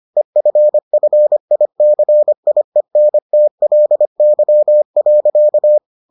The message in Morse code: EFFICIENTLY.

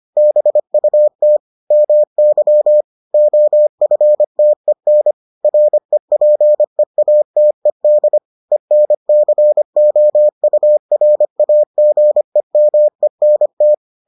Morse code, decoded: BUT MY OFTEN REPEATED ENCOURAGEMENT

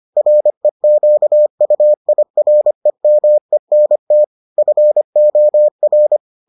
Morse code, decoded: REQUIREMENT FOR

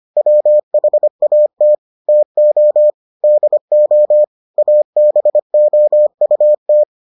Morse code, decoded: WHAT TO DO ABOUT

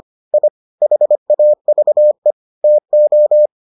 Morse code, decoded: I HAVE TO

